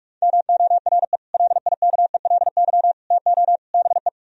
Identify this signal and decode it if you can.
Morse code, heard as a keyed tone, about 45 wpm, 700 Hz. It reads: MORE LIKELY TO BE